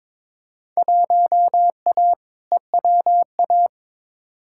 Morse code, heard as a keyed tone, about 21 wpm, 700 Hz. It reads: 1A EWA
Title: Field Day Exchanges Encore - 22wpm